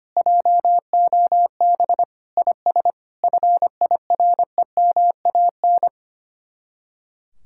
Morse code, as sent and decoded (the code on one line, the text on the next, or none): .--- --- -... / .. ... / ..-. .. .-. -- .- -.
JOB IS FIREMAN